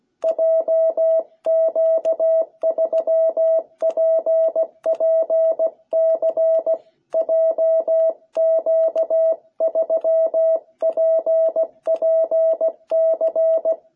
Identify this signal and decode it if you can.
Morse code, heard as a keyed tone, about 16 wpm, 660 Hz. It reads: JQ3PPC JQ3PPC